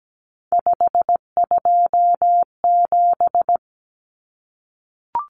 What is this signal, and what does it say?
Morse code, heard as a keyed tone, about 17 wpm, 700 Hz. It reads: 527
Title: Numbers - 3-Digit 17wpm